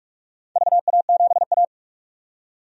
Morse code, heard as a keyed tone, about 46 wpm, 700 Hz. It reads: VA7A
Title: DXCC-100 Call Signs - 4 Characters - 45wpm